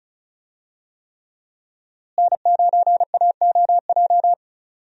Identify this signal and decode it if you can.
Morse code, heard as a keyed tone, about 36 wpm, 700 Hz. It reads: N9AOJ
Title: USA Call Signs - 5 Characters - Repeated 3x - 35wpm